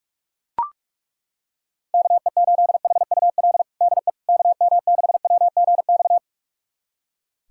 Morse code, heard as a keyed tone, about 42 wpm, 700 Hz. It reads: KE8HUL DE KM6WGX
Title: USA Callsign DE Callsign 45wpm